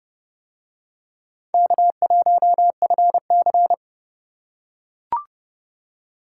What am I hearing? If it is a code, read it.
Morse code, heard as a keyed tone, about 29 wpm, 700 Hz. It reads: K1FC